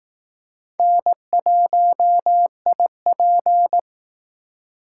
Morse code, heard as a keyed tone, about 18 wpm, 700 Hz. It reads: N1IP
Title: Speed Racing - USA Call Signs 15wpm